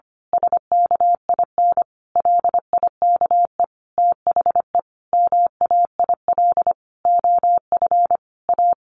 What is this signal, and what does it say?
Morse code, heard as a keyed tone, about 25 wpm, 700 Hz. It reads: SKIN LIKE THE MAIL OF A
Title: A Sound Of Thunder 25wpm